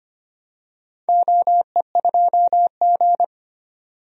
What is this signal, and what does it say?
Morse code, heard as a keyed tone, about 25 wpm, 700 Hz. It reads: OE2G
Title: DXCC-100 Call Signs - 4 Characters - 25wpm